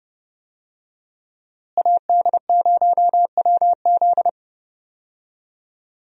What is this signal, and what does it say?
Morse code, heard as a keyed tone, about 32 wpm, 700 Hz. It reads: AD0WZ
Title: USA Call Signs - 5 Characters - Repeated 2x - 30wpm